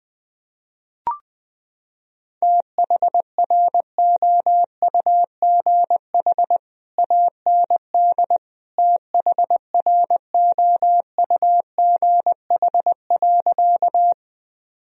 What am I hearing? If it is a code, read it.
Morse code, heard as a keyed tone, about 20 wpm, 700 Hz. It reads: THROUGH AND THROUGH.
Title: Sentences from Top 1000 Words - Review Part 1 - 20wpm